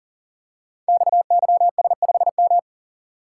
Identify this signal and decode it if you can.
Morse code, heard as a keyed tone, about 42 wpm, 700 Hz. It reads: XYS5M
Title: Code Groups - 5 Letter:Numbers - 40wpm